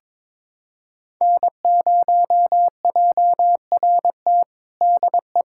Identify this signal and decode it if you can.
Morse code, heard as a keyed tone, about 22 wpm, 700 Hz. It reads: N0JRT DE
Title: USA Callsign DE Callsign - 5 Characters - 22wpm